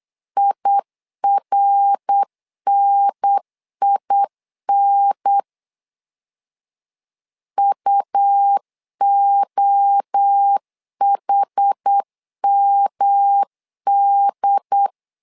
Morse code, heard as a keyed tone, about 8 wpm, 800 Hz. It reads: IRNIN UOHMD
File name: LEZIONI / 4 Q1 8wpm 3035